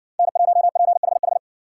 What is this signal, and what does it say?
Morse code, heard as a keyed tone, about 55 wpm, 700 Hz. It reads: N1PHH